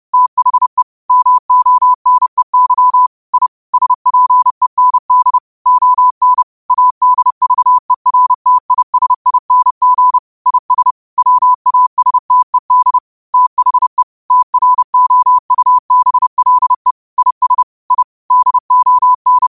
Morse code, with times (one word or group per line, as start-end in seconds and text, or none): 0.1-0.8 THE
1.1-3.1 MONEY
3.3-3.5 I
3.7-5.4 SPEND
5.7-6.4 ON
6.7-10.2 ADVERTISING
10.5-10.9 IS
11.2-13.0 WASTED
13.3-14.0 THE
14.3-16.9 TROUBLE
17.2-17.6 IS
17.9-18.0 I
18.3-19.5 DON